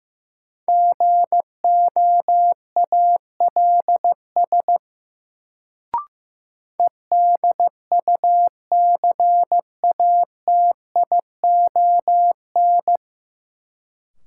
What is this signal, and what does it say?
Morse code, heard as a keyed tone, about 15 wpm, 700 Hz. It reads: GOALS EDUCATION
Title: Sentences from Top 1000 Words - Review Part 6 - New Word Drill - 15wpm